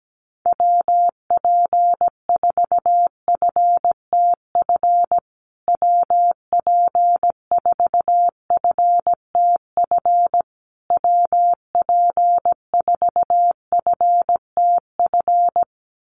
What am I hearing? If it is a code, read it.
Morse code, heard as a keyed tone, about 17 wpm, 700 Hz. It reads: WP4FTF WP4FTF WP4FTF
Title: USA Call Signs Repeated 3x 17wpm